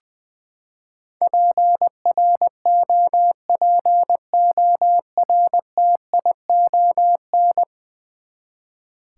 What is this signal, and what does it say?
Morse code, heard as a keyed tone, about 20 wpm, 700 Hz. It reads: PROPORTION
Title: Ten Letter Words - 20wpm